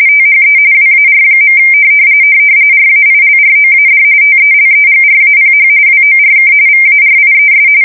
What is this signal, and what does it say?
RTTY, about 45.45 baud. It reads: THE WAR. RAILROAD TELEGRAPH WORK VARIES
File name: fsk170 / printing-telegraph